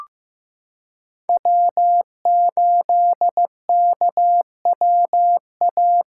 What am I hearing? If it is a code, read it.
Morse code, heard as a keyed tone, about 15 wpm, 700 Hz. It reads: W8KWA
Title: USA Callsign DE Callsign - 5 Characters - 15wpm